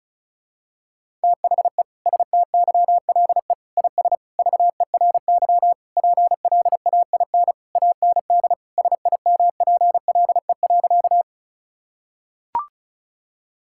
Morse code, heard as a keyed tone, about 36 wpm, 700 Hz. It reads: THE STYLE IS VERY PLAIN AND SIMPLE.
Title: Sentences from Top 1000 Words - Review Part 5 - 35wpm